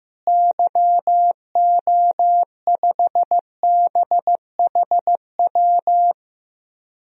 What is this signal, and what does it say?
Morse code, heard as a keyed tone, about 15 wpm, 700 Hz. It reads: YO5BHW
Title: DXCC-100 Call Signs - 6 Characters - Repeated 2x - 15wpm